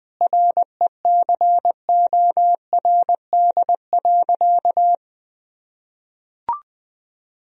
Morse code, heard as a keyed tone, about 21 wpm, 700 Hz. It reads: RECORD.
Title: Sentences from Top 1000 Words - Review Part 3 - 20wpm